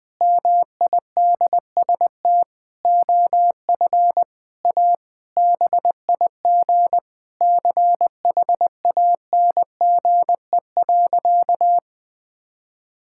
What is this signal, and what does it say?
Morse code, heard as a keyed tone, about 20 wpm, 700 Hz. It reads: MIDST OF A BIG CHANGE.
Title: Sentences with 7-Word Phrases - 20wpm